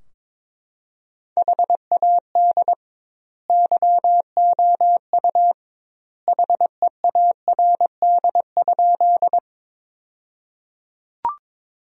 Morse code, heard as a keyed tone, about 22 wpm, 700 Hz. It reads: HAD YOU HEARD?
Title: 2x Word-Spacing - Sentences from Top 400 Words - 22wpm